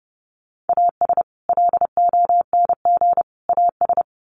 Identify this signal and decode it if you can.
Morse code, heard as a keyed tone, about 30 wpm, 700 Hz. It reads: AS LONG AS
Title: Sets of 3 Words 30wpm